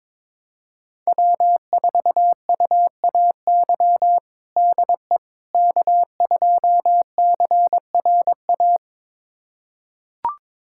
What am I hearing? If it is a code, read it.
Morse code, heard as a keyed tone, about 22 wpm, 700 Hz. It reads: W4UAY DE K2CRA
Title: USA Callsign DE Callsign - 5 Characters - 22wpm